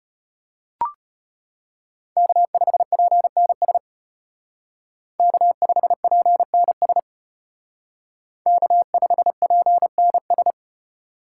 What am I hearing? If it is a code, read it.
Morse code, heard as a keyed tone, about 31 wpm, 700 Hz. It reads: K5PNS K5PNS K5PNS
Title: Speed Racing - USA 1x3 Call Signs 25wpm